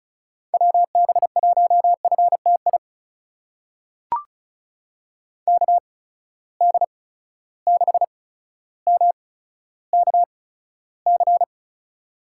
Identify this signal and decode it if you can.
Morse code, heard as a keyed tone, about 36 wpm, spaced as for 15 wpm, 700 Hz. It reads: WB1FTI KD6MKC
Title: USA Call Signs - 6 Characters - 8x Character Spacing - 35wpm